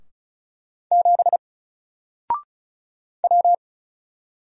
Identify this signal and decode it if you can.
Morse code, heard as a keyed tone, about 35 wpm, 700 Hz. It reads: 7 W
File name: Single-Letter-Number-35wpm